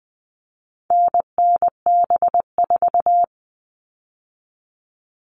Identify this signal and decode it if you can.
Morse code, heard as a keyed tone, about 20 wpm, 700 Hz. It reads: NNB4
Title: All Cut Numbers - 4-Digit 20wpm